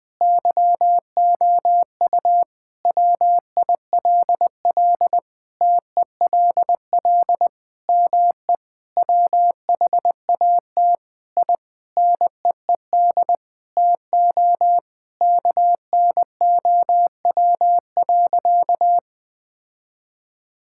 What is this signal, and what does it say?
Morse code, heard as a keyed tone, about 20 wpm, 700 Hz. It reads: YOU WILL TELL ME WHAT I NEED TO KNOW.
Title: Sentences from Top 1000 Words - Review Part 5 - 20wpm